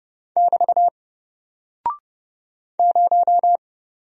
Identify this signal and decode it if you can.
Morse code, heard as a keyed tone, about 30 wpm, 700 Hz. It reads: <BT> 0
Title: Single Character (Letter:Number:Punctuation:Prosign) 30wpm